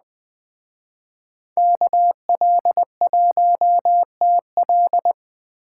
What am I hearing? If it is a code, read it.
Morse code, heard as a keyed tone, about 20 wpm, 700 Hz. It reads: KL1TL
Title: Speed Racing - USA Call Signs 17wpm